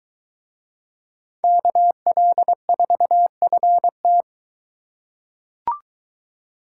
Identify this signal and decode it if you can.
Morse code, heard as a keyed tone, about 22 wpm, 700 Hz. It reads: KL4FT